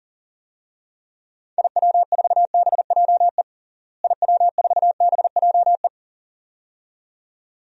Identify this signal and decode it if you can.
Morse code, heard as a keyed tone, about 37 wpm, 700 Hz. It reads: IW4BJE IW4BJE
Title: DXCC-100 Call Signs - 6 Characters - Repeated 2x - 40wpm